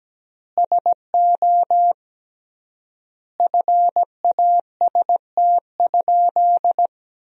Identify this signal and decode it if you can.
Morse code, heard as a keyed tone, about 17 wpm, 700 Hz. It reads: SO FAST?
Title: 3x Word-Spacing - Sentences from Top 500 Words - 17wpm